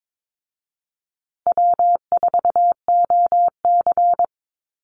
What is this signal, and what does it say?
Morse code, heard as a keyed tone, about 22 wpm, 700 Hz. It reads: W4OC
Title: Easy Call Signs Repeated 2x - 22wpm